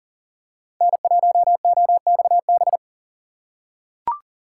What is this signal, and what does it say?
Morse code, heard as a keyed tone, about 40 wpm, 700 Hz. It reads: N1OXB
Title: USA Call Signs - 5 Characters - 3x Character Spacing - 40wpm